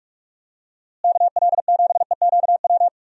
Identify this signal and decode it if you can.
Morse code, heard as a keyed tone, about 46 wpm, 700 Hz. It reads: KL7EQW
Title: DXCC-100 Call Signs - 6 Characters - Repeated 3x - 45wpm